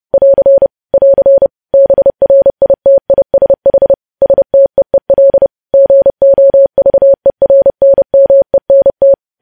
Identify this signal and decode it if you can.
Morse code, heard as a keyed tone, about 30 wpm, 550 Hz. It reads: <AR> <AR> BRITISH STEEL GOVERNMENT
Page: news headlines in morse code at 30 wpm